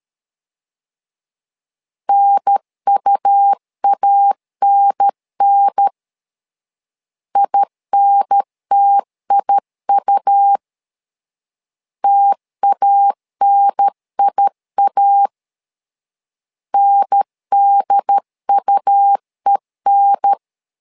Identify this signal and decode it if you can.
Morse code, heard as a keyed tone, about 13 wpm, 800 Hz. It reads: NUANN INTIU TANIA NDUEN
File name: LEZIONI / 2 UHDN / 2 UHDN 12wpm 1456